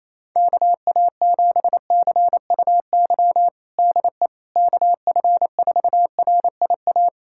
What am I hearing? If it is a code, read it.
Morse code, heard as a keyed tone, about 28 wpm, 700 Hz. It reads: KA7CUY DE KF4RIA
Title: USA Callsign DE Callsign 28wpm